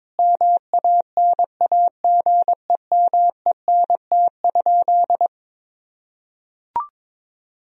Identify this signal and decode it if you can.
Morse code, heard as a keyed tone, about 22 wpm, 700 Hz. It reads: MANAGEMENT?
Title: Sentences from Top 1000 Words - Review Part 4 - 22wpm